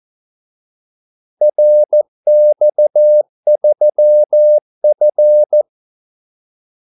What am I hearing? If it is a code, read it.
Morse code, heard as a keyed tone, about 14 wpm, 600 Hz. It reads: RX3F
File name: Callsigns-14wpm